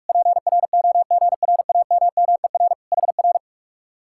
Morse code, heard as a keyed tone, about 59 wpm, spaced as for 52 wpm, 700 Hz. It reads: PROGRAMMER HR
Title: Speed-Racing Common QSO Phrases - 45wpm